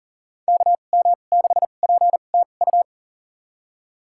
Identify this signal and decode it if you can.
Morse code, heard as a keyed tone, about 40 wpm, spaced as for 29 wpm, 700 Hz. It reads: KM6PTU
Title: USA Call Signs - 2x Character Spacing - 40wpm